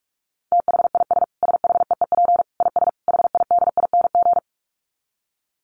Morse code, heard as a keyed tone, about 44 wpm, 700 Hz. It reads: THIS SHEEP IS HIDING
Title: Dit Challenge 45wpm